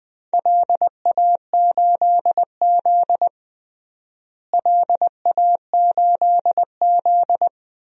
Morse code, heard as a keyed tone, about 20 wpm, 700 Hz. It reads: LA8Z LA8Z
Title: DXCC-100 Call Signs - 4 Characters - Repeated 2x - 20wpm